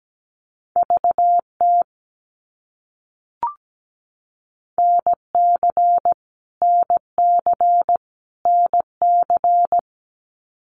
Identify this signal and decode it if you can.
Morse code, heard as a keyed tone, about 17 wpm, 700 Hz. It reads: VT NC NC NC
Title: US States Repeated 3x 17wpm